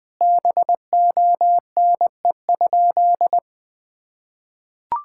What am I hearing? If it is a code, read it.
Morse code, heard as a keyed tone, about 20 wpm, 700 Hz. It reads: BONE?
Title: Sentences from Top 1600 Words 20wpm